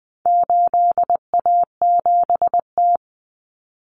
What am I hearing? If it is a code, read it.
Morse code, heard as a keyed tone, about 20 wpm, 700 Hz. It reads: 8A7T
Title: Common Cut Numbers - 4-Digit 20wpm